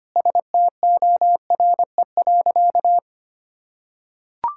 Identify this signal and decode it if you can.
Morse code, heard as a keyed tone, about 25 wpm, 700 Hz. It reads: STORE.
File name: practice / 25wpm / Sentences with 2-Word Phrases - 25wpm